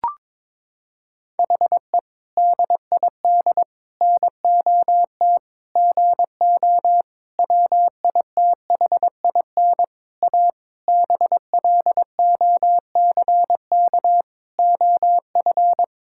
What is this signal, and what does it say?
Morse code, heard as a keyed tone, about 22 wpm, 700 Hz. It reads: HE DID NOT GO WITHIN A BLOCK OF
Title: Sentences from Top 1600 Words 22wpm